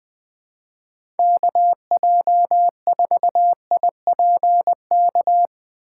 Morse code, heard as a keyed tone, about 20 wpm, 700 Hz. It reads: KJ4IPK
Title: USA Call Signs - 4x Character Spacing - 20wpm